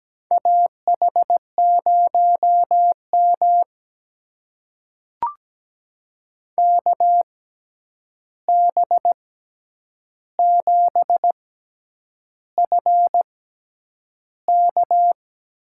Morse code, heard as a keyed tone, about 17 wpm, spaced as for 8 wpm, 700 Hz. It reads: AH0M KB7FK
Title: USA Call Signs - 6x Character Spacing - 17wpm